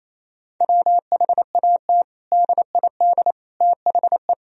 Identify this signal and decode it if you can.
Morse code, heard as a keyed tone, about 28 wpm, 700 Hz. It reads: WHAT DID THE
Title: Sentences from Top 1000 Words - Review Part 4 - 28wpm